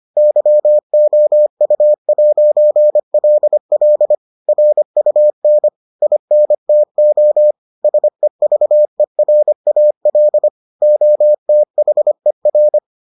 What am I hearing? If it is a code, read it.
Morse code, heard as a keyed tone, about 25 wpm, 600 Hz. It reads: YOU'LL RUN INTO SEVERAL OTHER